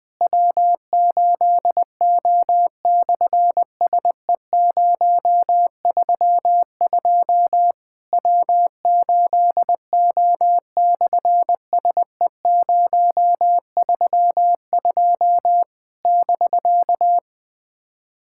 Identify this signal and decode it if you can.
Morse code, heard as a keyed tone, about 20 wpm, 700 Hz. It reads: W8O/SE032 W8O/SE032 <BK>
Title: SOTA QSO Practice 20wpm